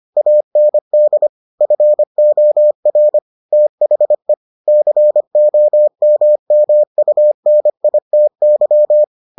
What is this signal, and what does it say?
Morse code, heard as a keyed tone, about 25 wpm, 600 Hz. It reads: AND FOR THE COMMUNITY